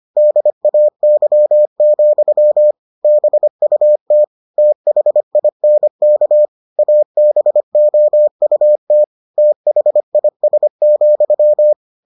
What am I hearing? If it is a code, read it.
Morse code, heard as a keyed tone, about 25 wpm, 600 Hz. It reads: DAY, BUT THINK ABOUT THIS,